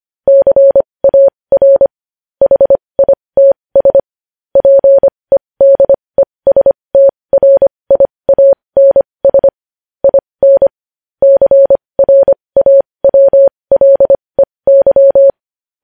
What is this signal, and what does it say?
Morse code, heard as a keyed tone, about 25 wpm, spaced as for 20 wpm, 550 Hz. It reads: CAR HITS PEDESTRIANS IN CRAWLEY
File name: podcast / News headlines-1x-25x20